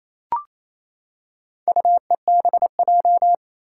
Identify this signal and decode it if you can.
Morse code, heard as a keyed tone, about 27 wpm, 700 Hz. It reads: UEBJ
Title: Four Challenging Letter-Number Combinations 28wpm